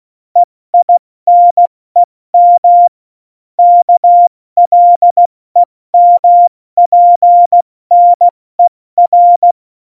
Morse code, heard as a keyed tone, about 16 wpm, 700 Hz. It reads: EINEM KLEMPNER
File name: Morsekurs / Witze / Witz 01. September 2020, 70 F80